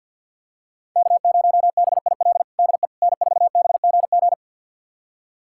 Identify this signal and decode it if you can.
Morse code, heard as a keyed tone, about 51 wpm, 700 Hz. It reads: K0BIL DE N4BGG
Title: USA Callsign DE Callsign - 5 Characters - 50wpm